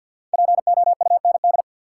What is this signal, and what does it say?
Morse code, heard as a keyed tone, about 52 wpm, 700 Hz. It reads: POUND
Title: Five Letter Words - 50wpm